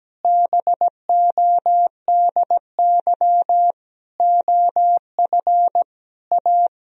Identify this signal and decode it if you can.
Morse code, heard as a keyed tone, about 17 wpm, 700 Hz. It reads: BODY OF A
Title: Sentences from Top 1000 Words - Review Part 2 - 17wpm